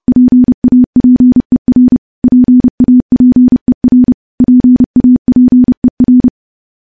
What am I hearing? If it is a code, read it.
Morse code, heard as a keyed tone, about 30 wpm, 250 Hz. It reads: PAPER PAPER PAPER
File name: morse / morse-training-30WPM